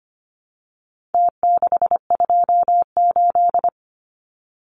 Morse code, heard as a keyed tone, about 25 wpm, 700 Hz. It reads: T628
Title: Numbers - Contest Style - 4-Digit 25wpm